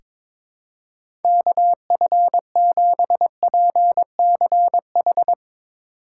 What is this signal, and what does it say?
Morse code, heard as a keyed tone, about 22 wpm, 700 Hz. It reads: KF7PCH
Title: USA Call Signs - 6 Characters - 3x Character Spacing - 22wpm